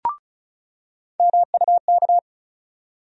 Morse code, heard as a keyed tone, about 36 wpm, 700 Hz. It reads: MUK
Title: USA Call Sign Postfixes - 3 Letters 35wpm